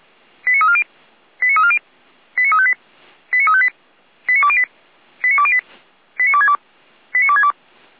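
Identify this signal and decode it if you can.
Five-tone selective call (ZVEI), heard as quick runs of five tones, tones of about 70 ms. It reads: 89340 89340 89358 89358 89208 89208 89262 89262
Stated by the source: Received on SDR as a ZVEI five-tone selective call, address 89303.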